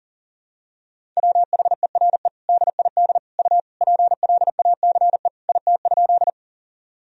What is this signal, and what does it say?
Morse code, heard as a keyed tone, about 40 wpm, 700 Hz. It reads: WHERE DID U PLACE IT?
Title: Sentences from Top 200 Words and Abbreviations 40wpm